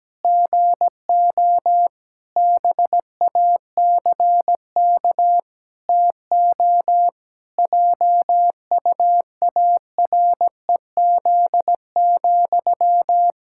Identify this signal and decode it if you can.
Morse code, heard as a keyed tone, about 17 wpm, 700 Hz. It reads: GO BACK TO JUAREZ,
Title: Pangram Challenge v2 - 17wpm